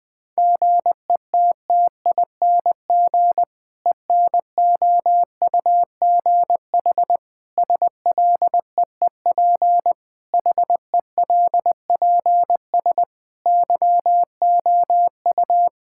Morse code, heard as a keyed tone, about 20 wpm, 700 Hz. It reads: GETTING ENOUGH SLEEP HELPS YOU
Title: Difficult Sentences with Hints - 20wpm